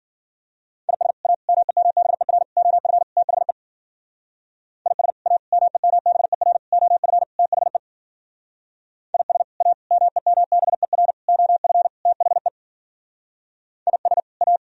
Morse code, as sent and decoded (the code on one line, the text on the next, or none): .. ... / .- / -- . -- -... .-. / --- ..-. / - .... / .. ... / .- / -- . -- -... .-. / --- ..-. / - .... / .. ... / .- / -- . -- -... .-. / --- ..-. / - .... / .. ... / .-
IS A MEMBER OF THE IS A MEMBER OF THE IS A MEMBER OF THE IS A